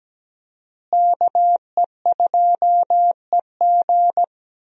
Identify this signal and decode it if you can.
Morse code, heard as a keyed tone, about 17 wpm, 700 Hz. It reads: KE2EG
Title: USA Call Signs - 5 Characters - Repeated 2x - 17wpm